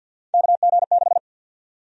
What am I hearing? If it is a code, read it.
Morse code, heard as a keyed tone, about 51 wpm, 700 Hz. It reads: KG6
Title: USA Call Sign Prefixes - 50wpm